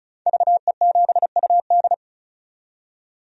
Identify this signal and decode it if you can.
Morse code, heard as a keyed tone, about 34 wpm, 700 Hz. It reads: VE7UD
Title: DXCC-100 Call Signs - 5 Characters - 35wpm